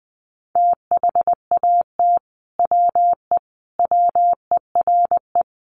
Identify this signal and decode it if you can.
Morse code, heard as a keyed tone, about 20 wpm, 700 Hz. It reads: THAT WE WERE
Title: Sets of 5 Words Encore2 20wpm